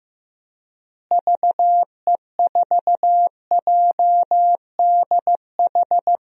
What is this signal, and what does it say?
Morse code, heard as a keyed tone, about 15 wpm, 700 Hz. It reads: VE4JDH
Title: DXCC-100 Call Signs - 6 Characters - Repeated 2x - 15wpm